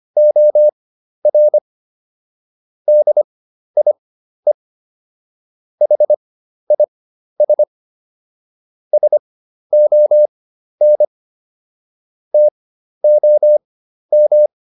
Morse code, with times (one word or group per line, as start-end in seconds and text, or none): 0.2-1.6 OR
2.9-4.5 DIE
5.8-7.6 HIS
8.9-11.0 SON
12.3-14.5 TOM